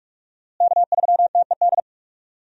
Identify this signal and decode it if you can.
Morse code, heard as a keyed tone, about 46 wpm, 700 Hz. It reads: K3TED
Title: USA Call Signs - 5 Characters - Repeated 3x - 45wpm